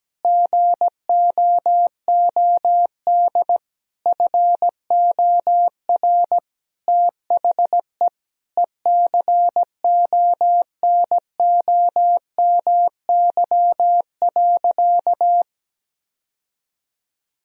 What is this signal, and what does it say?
Morse code, heard as a keyed tone, about 17 wpm, 700 Hz. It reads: GOOD FOR THE ECONOMY.
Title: Sentences from Top 1000 Words - Review Part 4 - 17wpm